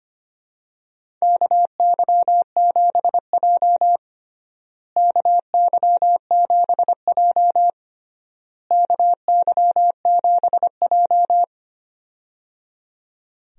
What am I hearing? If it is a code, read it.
Morse code, heard as a keyed tone, about 25 wpm, 700 Hz. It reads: KY7J KY7J KY7J
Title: USA Call Signs - 4 Characters - Repeated 3x - 25wpm